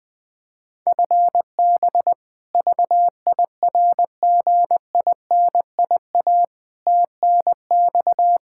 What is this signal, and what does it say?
Morse code, heard as a keyed tone, about 20 wpm, 700 Hz. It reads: FB VIRGINIA TNX
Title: K1USN SST QSO Practice 20wpm